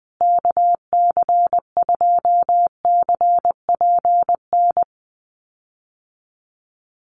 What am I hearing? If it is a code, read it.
Morse code, heard as a keyed tone, about 20 wpm, 700 Hz. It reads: KC2CPN
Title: USA Call Signs Repeated 3x 20wpm